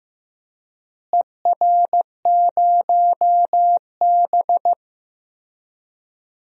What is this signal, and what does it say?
Morse code, heard as a keyed tone, about 15 wpm, 700 Hz. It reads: ER0B
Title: DXCC-100 Call Signs - 4 Characters - Repeated 3x - 15wpm